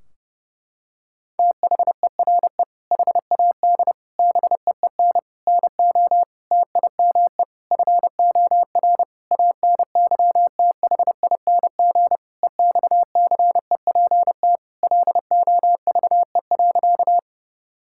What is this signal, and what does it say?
Morse code, heard as a keyed tone, about 32 wpm, 700 Hz. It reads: THERE HAD BEEN NO TIME FOR ANYTHING EXCEPT LOVE.
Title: Sentences from Top 1600 Words 30wpm